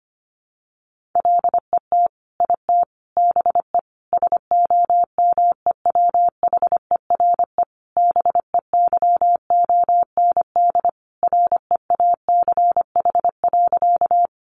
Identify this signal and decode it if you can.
Morse code, heard as a keyed tone, about 25 wpm, 700 Hz. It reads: LET IT BE SOMEWHERE BEYOND REACH.
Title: Sentences from Top 1400 Words 25wpm